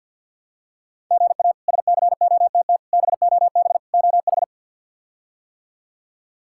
Morse code, heard as a keyed tone, about 50 wpm, 700 Hz. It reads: GA SCOTT BOB OH
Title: K1USN SST QSO Practice - 50wpm